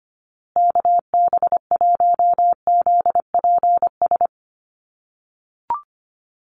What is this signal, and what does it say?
Morse code, heard as a keyed tone, about 25 wpm, 700 Hz. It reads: KB1ZPS
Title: USA Call Signs Repeated 2x 25wpm